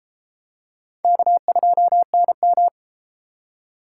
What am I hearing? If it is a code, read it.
Morse code, heard as a keyed tone, about 30 wpm, 700 Hz. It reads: K2NM